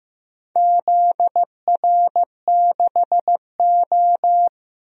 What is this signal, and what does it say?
Morse code, heard as a keyed tone, about 15 wpm, 700 Hz. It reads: ZR6O
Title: DXCC-100 Call Signs - 4 Characters - 15wpm